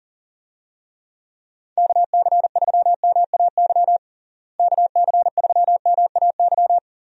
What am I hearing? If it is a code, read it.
Morse code, heard as a keyed tone, about 40 wpm, 700 Hz. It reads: KC3MAY KC3MAY
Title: USA Call Signs - 6 Characters - Repeated 3x - 40wpm